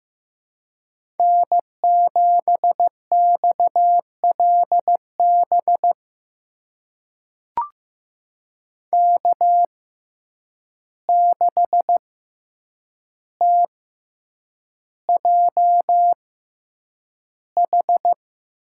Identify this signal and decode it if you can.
Morse code, heard as a keyed tone, about 15 wpm, spaced as for 8 wpm, 700 Hz. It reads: N7XLB K6TJH
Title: USA Call Signs - 5 Characters - 6x Character Spacing - 15wpm